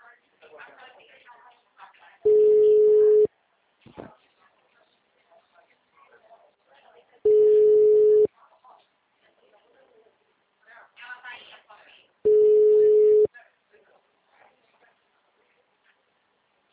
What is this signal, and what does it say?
Ringback tone (Europe, 425 Hz), with speech.
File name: out- -2018- - - .80335